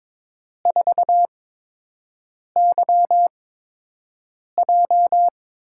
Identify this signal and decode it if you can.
Morse code, heard as a keyed tone, about 22 wpm, spaced as for 6 wpm, 700 Hz. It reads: 4YJ
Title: USA Call Signs - 4 Characters - 8x Character Spacing - 22wpm